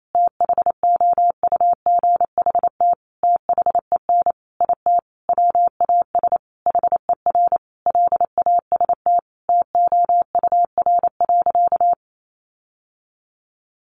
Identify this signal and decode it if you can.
Morse code, heard as a keyed tone, about 28 wpm, 700 Hz. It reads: THOUGHT THEN IT WAS HER LAST TOUR.
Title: Sentences from Top 1400 Words 28wpm